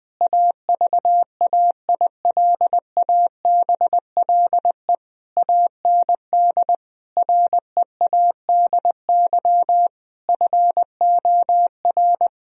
Morse code, heard as a keyed tone, about 20 wpm, 700 Hz. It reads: AVAILABLE AND READY FOR
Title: Sentences from Top 1000 Words - Review Part 3 - 20wpm